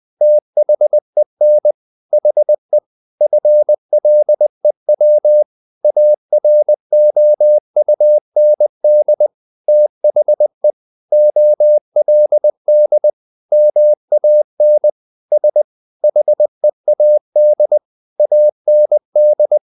Morse code, heard as a keyed tone, about 20 wpm, 600 Hz. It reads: THEN HE FLEW AROUND THE OLD MAN S HEAD AND
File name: the old man and the sea / the old man and the sea part 1 20wpm